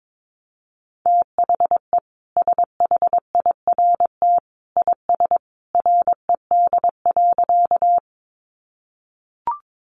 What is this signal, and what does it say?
Morse code, heard as a keyed tone, about 22 wpm, 700 Hz. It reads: THE SHIRT IS RED.
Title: Additional Easy Phrases 22wpm